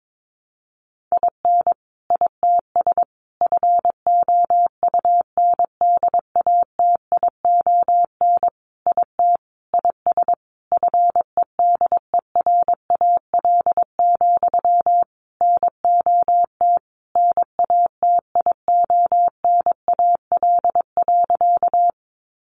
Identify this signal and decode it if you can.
Morse code, heard as a keyed tone, about 22 wpm, 700 Hz. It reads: IN ITS FOUNDATION IT IS FEDERAL, NOT NATIONAL.
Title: Sentences from Top 1200 Words 22wpm